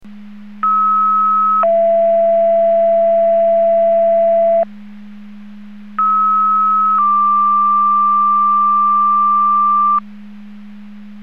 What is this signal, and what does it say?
Two-tone page, heard as a one-second tone, then a three-second tone, about 1290 then 670 Hz, about 1290 then 1180 Hz. It is heard from 0.6 to 10.0 s.